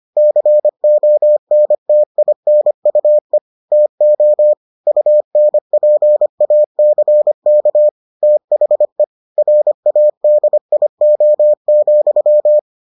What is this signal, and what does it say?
Morse code, heard as a keyed tone, about 25 wpm, 600 Hz. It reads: CONTINUE TO UNPACK THE RADIO,